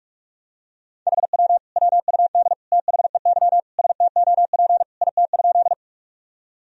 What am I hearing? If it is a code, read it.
Morse code, heard as a keyed tone, about 46 wpm, 700 Hz. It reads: HW WUD THEY STOP IT?
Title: Sentences from Top 300 Words and Abbreviations 45wpm